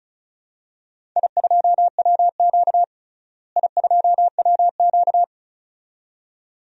Morse code, heard as a keyed tone, about 36 wpm, 700 Hz. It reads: I2WQ I2WQ
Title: DXCC-100 Call Signs - 4 Characters - Repeated 2x - 35wpm